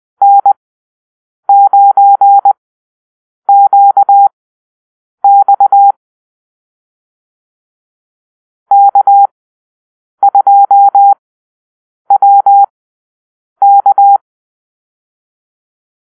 Morse code, heard as a keyed tone, about 20 wpm, spaced as for 7 wpm, 800 Hz. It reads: N9QX K2WK
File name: arrl / Call-Sign Drill-10